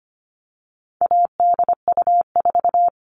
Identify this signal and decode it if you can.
Morse code, heard as a keyed tone, about 25 wpm, 700 Hz. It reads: ADU4
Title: All Cut Numbers - 4-Digit 25wpm